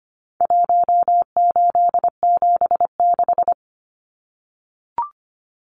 Morse code, heard as a keyed tone, about 25 wpm, 700 Hz. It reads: 1876
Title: Numbers - Contest Style - 4-Digit 25wpm